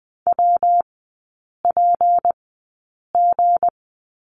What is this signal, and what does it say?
Morse code, heard as a keyed tone, about 20 wpm, spaced as for 8 wpm, 700 Hz. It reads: WPG
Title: 212 Lesson 34e - Number 8 - Three Characters 20wpm